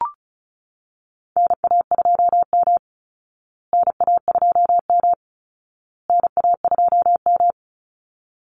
Morse code, heard as a keyed tone, about 34 wpm, 700 Hz. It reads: NA2M NA2M NA2M
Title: Call Signs Repeated 2x - 35wpm